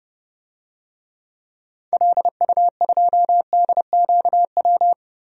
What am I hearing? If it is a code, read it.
Morse code, heard as a keyed tone, about 32 wpm, 700 Hz. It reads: LU2DQW
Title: DXCC-100 Call Signs - 6 Characters - Repeated 3x - 30wpm